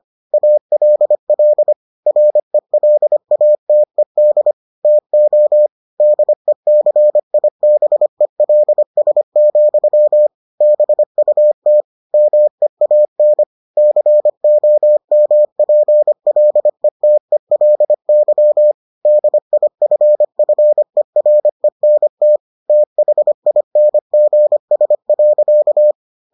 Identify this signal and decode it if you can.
Morse code, heard as a keyed tone, about 25 wpm, 600 Hz. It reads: ALL RELATED TO DECIBELS, BUT MEAN COMPLETELY DIFFERENT THINGS.